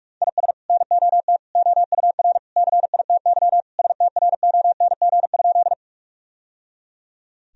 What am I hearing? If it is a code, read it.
Morse code, heard as a keyed tone, about 46 wpm, 700 Hz. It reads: IS NOT OUR CITY STRONG?
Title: Sentences from Top 1000 Words - Review Part 3 - 45wpm